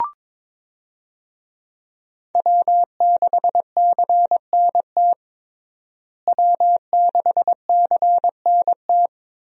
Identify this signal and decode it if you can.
Morse code, heard as a keyed tone, about 22 wpm, 700 Hz. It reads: W6CNT W6CNT
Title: USA Call Signs - 5 Characters - Repeated 2x - 22wpm